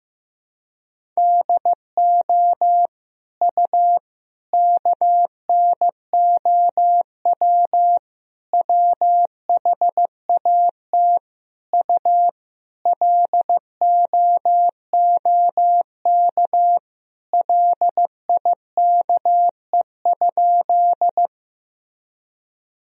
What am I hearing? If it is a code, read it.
Morse code, heard as a keyed tone, about 15 wpm, 700 Hz. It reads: DO U KNOW WHAT U LOOK LIKE?
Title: Sentences from Top 100 Words and Abbreviations 15wpm